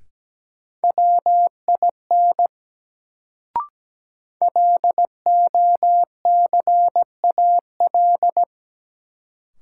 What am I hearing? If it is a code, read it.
Morse code, heard as a keyed tone, about 17 wpm, 700 Hz. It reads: WIN LOCAL
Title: Sentences from Top 1500 Words - New Word Drill Alt - 17wpm